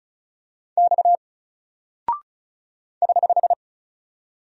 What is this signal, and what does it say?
Morse code, heard as a keyed tone, about 36 wpm, 700 Hz. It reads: X <HH>
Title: Single Character (Letter:Number:Punctuation:Prosign) Alt 35wpm